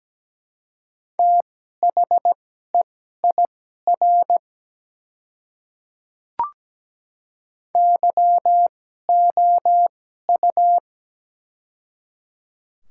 Morse code, heard as a keyed tone, about 17 wpm, spaced as for 12 wpm, 700 Hz. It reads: THEIR YOU
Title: 2x Character Spacing - Top 100 Words - 17wpm